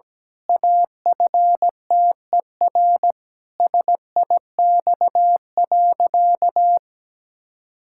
Morse code, heard as a keyed tone, about 17 wpm, 700 Hz. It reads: AFTER SIX.
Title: Pangram Challenge v2 - 17wpm